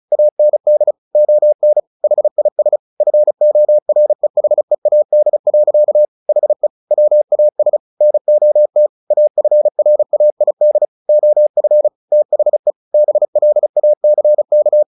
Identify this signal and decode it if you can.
Morse code, heard as a keyed tone, about 34 wpm, 600 Hz. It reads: AND ON HIS FOREHEAD. HE WAS NOT AFRAID OF THE BLACK
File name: the old man and the sea 2 35wpm